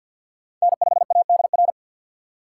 Morse code, heard as a keyed tone, about 50 wpm, 700 Hz. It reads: N5ADR
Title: USA Call Signs - 5 Characters - 3x Character Spacing - 50wpm